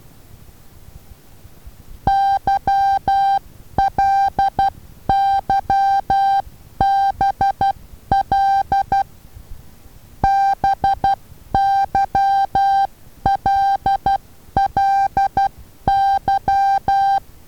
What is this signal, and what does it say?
Morse code, heard as a keyed tone, about 12 wpm, 790 Hz. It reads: YLYBL BYLLY
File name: Taviro HA6VA B